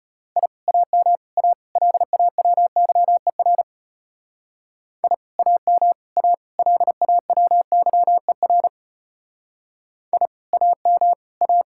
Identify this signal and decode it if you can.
Morse code, heard as a keyed tone, about 31 wpm, 700 Hz. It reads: I AM A LAWYER I AM A LAWYER I AM A